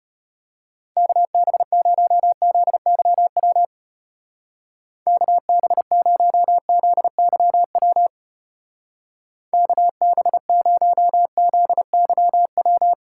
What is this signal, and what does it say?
Morse code, heard as a keyed tone, about 31 wpm, 700 Hz. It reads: KB0ZYW KB0ZYW KB0ZYW
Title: Speed Racing - USA 2x3 Call Signs 25wpm